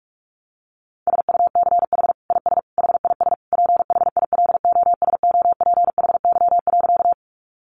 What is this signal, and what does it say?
Morse code, heard as a keyed tone, about 44 wpm, 700 Hz. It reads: SUCH IS HIS PHILOSOPHY.
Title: Sentences from Top 900 Words 45wpm